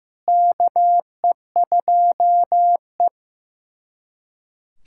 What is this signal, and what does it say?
Morse code, heard as a keyed tone, about 15 wpm, 700 Hz. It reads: KE2E
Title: USA Callsign DE Callsign - 4 Characters - 15wpm